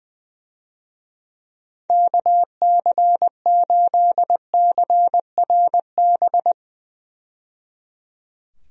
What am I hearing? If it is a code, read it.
Morse code, heard as a keyed tone, about 20 wpm, 700 Hz. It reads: KC8CRB
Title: USA Call Signs - 6 Characters - 20wpm